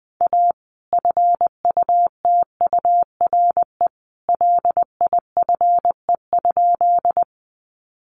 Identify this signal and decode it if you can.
Morse code, heard as a keyed tone, about 20 wpm, 700 Hz. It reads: A FUTURE LIFE?
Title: Sentences from Top 700 Words 20wpm